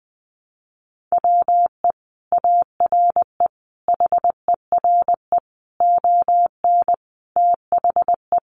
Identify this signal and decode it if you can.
Morse code, heard as a keyed tone, about 20 wpm, 700 Hz. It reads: WE ARE HERE ON THE
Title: Sentences from Top 1400 Words 20wpm